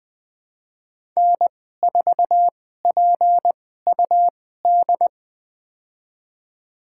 Morse code, heard as a keyed tone, about 20 wpm, spaced as for 14 wpm, 700 Hz. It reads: N4PUD
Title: USA Call Signs - 2x Character Spacing - 20wpm